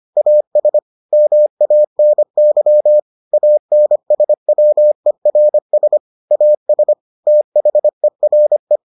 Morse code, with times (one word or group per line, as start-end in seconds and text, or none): 0.2-0.8 AS
1.1-3.0 MANY
3.3-6.0 ANSWERS
6.3-6.9 AS
7.3-8.8 THERE